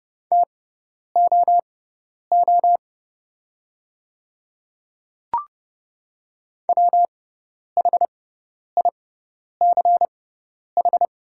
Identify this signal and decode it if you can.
Morse code, heard as a keyed tone, about 30 wpm, spaced as for 10 wpm, 700 Hz. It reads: TOO WHICH
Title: 6x Character Spacing - Top 200 Words - 30wpm